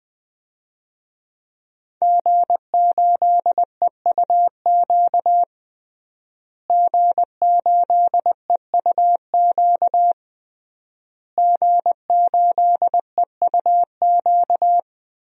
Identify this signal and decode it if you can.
Morse code, heard as a keyed tone, about 20 wpm, 700 Hz. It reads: G8EUQ G8EUQ G8EUQ
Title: DXCC-100 Call Signs - 5 Characters - Repeated 3x - 20wpm